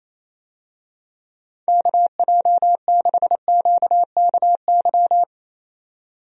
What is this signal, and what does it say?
Morse code, heard as a keyed tone, about 27 wpm, 700 Hz. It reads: KJ6QKY
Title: USA Call Signs - 6 Characters - 28wpm